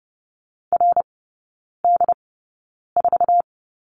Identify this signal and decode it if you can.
Morse code, heard as a keyed tone, about 30 wpm, spaced as for 9 wpm, 700 Hz. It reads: RD4
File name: course / 30:10wpm / 087 Lesson 15e - Number 5 - Three Characters 30wpm